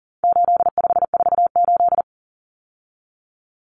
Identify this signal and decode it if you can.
Morse code, heard as a keyed tone, about 40 wpm, 700 Hz. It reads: 8548
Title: Numbers - 4-Digit 40wpm